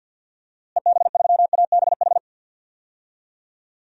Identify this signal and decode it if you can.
Morse code, heard as a keyed tone, about 50 wpm, 700 Hz. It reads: EB3ABH
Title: DXCC-100 Call Signs - 6 Characters - Repeated 3x - 50wpm